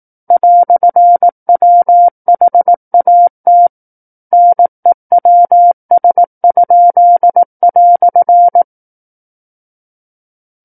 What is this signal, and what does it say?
Morse code, heard as a keyed tone, about 18 wpm, 700 Hz. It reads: "WHAT NEWS?"
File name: 18 WPM